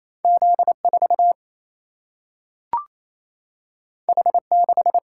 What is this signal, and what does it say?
Morse code, heard as a keyed tone, about 28 wpm, 700 Hz. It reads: Z 4 H 6
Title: Single-Letter Single-Number 28wpm